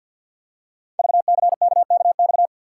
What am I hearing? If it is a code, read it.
Morse code, heard as a keyed tone, about 52 wpm, 700 Hz. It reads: VCKKX
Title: Code Groups - 5 Letters - 50wpm